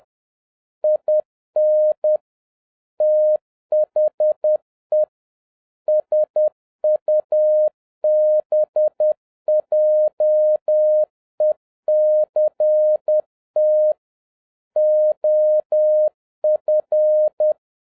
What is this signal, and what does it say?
Morse code, heard as a keyed tone, about 10 wpm, 610 Hz. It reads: IN THE SUBJECT OF